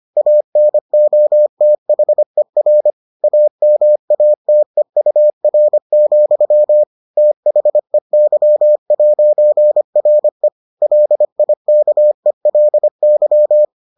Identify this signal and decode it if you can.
Morse code, heard as a keyed tone, about 25 wpm, 600 Hz. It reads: ANOTHER AMATEUR, THEY'RE LIKELY